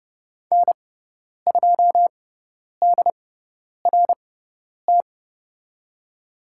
Morse code, heard as a keyed tone, about 30 wpm, spaced as for 10 wpm, 700 Hz. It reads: N2DRT
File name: Lesson 16g - Letter c - Call Signs 30wpm